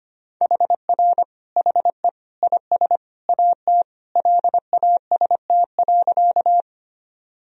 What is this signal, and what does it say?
Morse code, heard as a keyed tone, about 25 wpm, 700 Hz. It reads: HR HE IS AT LAST.
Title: Sentences from Top 300 Words and Abbreviations 25wpm